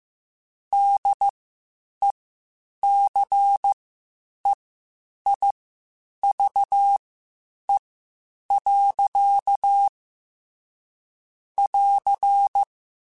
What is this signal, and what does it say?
Morse code, heard as a keyed tone, about 15 wpm, spaced as for 8 wpm, 800 Hz. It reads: DECEIVE. <AR>
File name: Quotes Spaced / Quotes Spaced-mp3s / Deceive sp @13